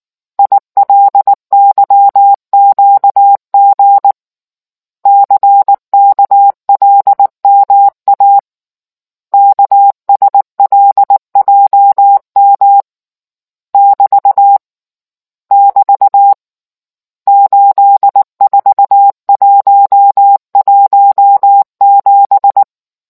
Morse code, heard as a keyed tone, about 19 wpm, 800 Hz. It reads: ILYQG CKLMA KSLJM <BT> <BT> 84117